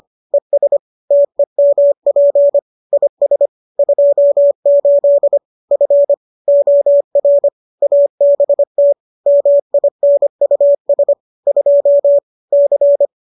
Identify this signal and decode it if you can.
Morse code, heard as a keyed tone, about 25 wpm, 570 Hz. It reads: ES TEMP IS 28 F OR ABT MINUS 2 C